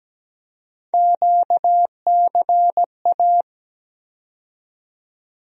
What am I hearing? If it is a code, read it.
Morse code, heard as a keyed tone, about 17 wpm, 700 Hz. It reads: QCA
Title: USA Call Sign Postfixes - 3 Letters 17wpm